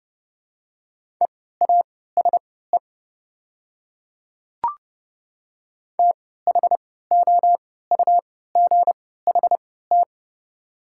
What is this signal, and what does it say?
Morse code, heard as a keyed tone, about 30 wpm, spaced as for 17 wpm, 690 Hz. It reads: EASE THOUGHT
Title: 3x Character Spacing - Top 300 Words - 30wpm